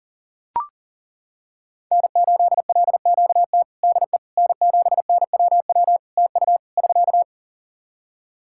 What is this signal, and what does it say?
Morse code, heard as a keyed tone, about 40 wpm, 700 Hz. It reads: N8LQT DE N7NWW TU <SK>
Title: Practice QSO - Exchange 4 - Ending - 40wpm